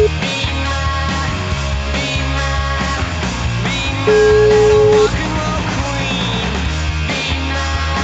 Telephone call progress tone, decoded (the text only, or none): ringback tone